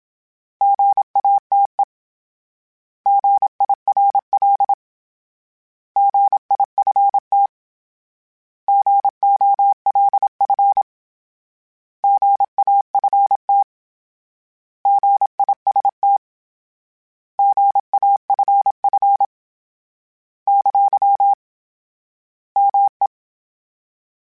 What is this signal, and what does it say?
Morse code, heard as a keyed tone, about 27 wpm, 800 Hz. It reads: GATE GIRL GIFT GOLF GAFT GIST GAFF ! ME
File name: Misc / Misc-mp3s / Misc @25